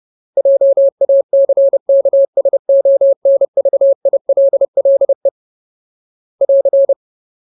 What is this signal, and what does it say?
Morse code, heard as a keyed tone, about 30 wpm, 550 Hz. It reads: JACKSONVILLE <AR>
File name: Ultimate Word List / 30WPM Ultimate